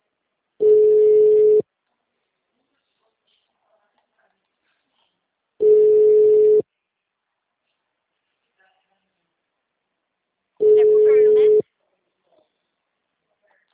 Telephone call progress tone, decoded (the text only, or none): ringback tone